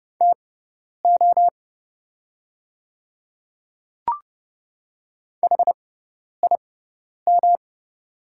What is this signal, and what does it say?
Morse code, heard as a keyed tone, about 31 wpm, spaced as for 10 wpm, 690 Hz. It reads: TO HIM
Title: 6x Character Spacing - Top 100 Words - 30wpm